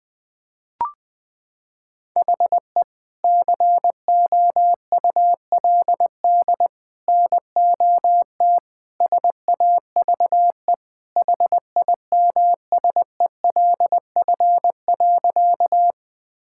Morse code, heard as a keyed tone, about 21 wpm, 700 Hz. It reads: HE COULD NOT SAVE HIMSELF.
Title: Sentences from Top 1600 Words 20wpm